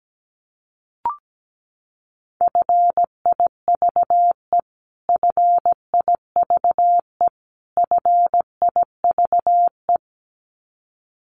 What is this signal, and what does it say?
Morse code, heard as a keyed tone, about 17 wpm, 700 Hz. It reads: FIVE FIVE FIVE
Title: Top 500 Words Repeated 17wpm